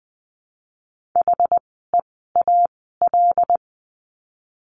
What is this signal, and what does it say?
Morse code, heard as a keyed tone, about 20 wpm, spaced as for 14 wpm, 700 Hz. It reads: HEAL
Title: 075 Lesson 13f - Letter L - Words 20wpm